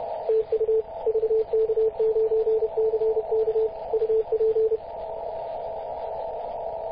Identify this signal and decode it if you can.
Morse code, heard as a keyed tone, about 31 wpm, 440 Hz, with radio noise. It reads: TU VK9CK UP
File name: posts / 2016-mar-19-0241z-vk9ck-21-032mhz-cw